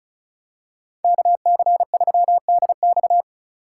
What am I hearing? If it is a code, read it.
Morse code, heard as a keyed tone, about 36 wpm, 700 Hz. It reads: KC3DX
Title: USA Call Signs - 5 Characters - 2x Character Spacing - 35wpm